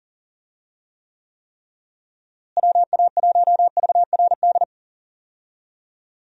Morse code, heard as a keyed tone, about 40 wpm, 700 Hz. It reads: WA1VRD
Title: USA Call Signs - 6 Characters - Repeated 3x - 40wpm